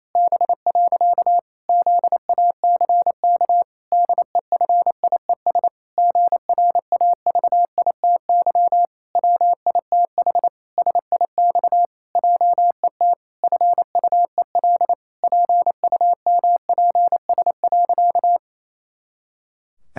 Morse code, heard as a keyed tone, about 28 wpm, 700 Hz. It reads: B. ZACK DEFIES GRAVITY WITH SIX JET FUEL PUMPS.
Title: Pangram Challenge v2 - 28wpm